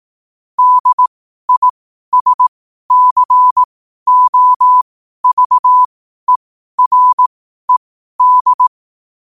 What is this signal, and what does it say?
Morse code, heard as a keyed tone, about 18 wpm, spaced as for 12 wpm, 1000 Hz. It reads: DISCOVERED
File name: qotdfiles / 12wpm